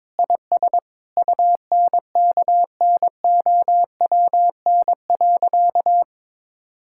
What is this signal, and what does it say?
Morse code, heard as a keyed tone, about 22 wpm, 700 Hz. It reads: IS UNKNOWN.
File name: practice / 22wpm / Sentences with 4-Word Phrases - 22wpm